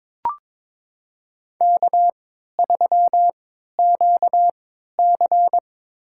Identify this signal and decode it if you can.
Morse code, heard as a keyed tone, about 22 wpm, spaced as for 13 wpm, 700 Hz. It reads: K3QC
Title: USA Call Signs - 4 Characters - 3x Character Spacing - 22wpm